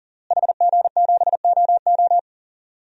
Morse code, heard as a keyed tone, about 40 wpm, 700 Hz. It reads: HG7OO
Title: DXCC-100 Call Signs - 5 Characters - Repeated 3x - 40wpm